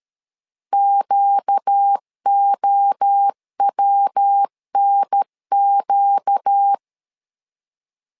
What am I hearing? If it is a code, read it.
Morse code, heard as a keyed tone, about 13 wpm, 800 Hz. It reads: QOWNQ
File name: LEZIONI / 4 Q 12wpm 1833